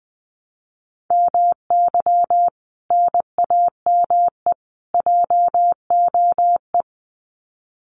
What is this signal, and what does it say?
Morse code, heard as a keyed tone, about 20 wpm, 700 Hz. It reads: MY NAME JOE
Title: Common QSO Phrases 20wpm